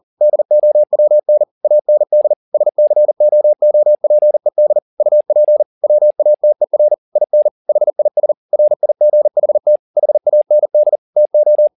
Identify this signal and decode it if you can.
Morse code, heard as a keyed tone, about 40 wpm, 600 Hz. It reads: DOWN AND SCOOPED UP WATER IN HIS RIGHT HAND TO
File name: the old man and the sea 2 40wpm